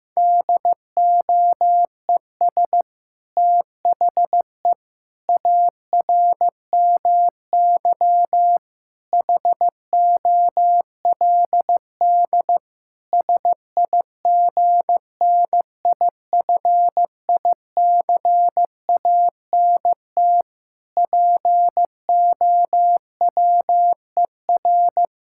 Morse code, heard as a keyed tone, about 15 wpm, 700 Hz. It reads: DOES THE ARMY HOLD SIGNIFICANT POWER
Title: Sentences from Top 1000 Words - Review Part 6 - 15wpm